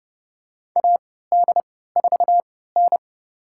Morse code, heard as a keyed tone, about 30 wpm, spaced as for 17 wpm, 700 Hz. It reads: AD4N
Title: USA Call Signs - 4 Characters - 3x Character Spacing - 30wpm